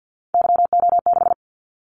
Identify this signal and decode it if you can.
Morse code, heard as a keyed tone, about 50 wpm, 700 Hz. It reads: YO6
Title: Callsign Prefixes for Top 25 DXCC - 50wpm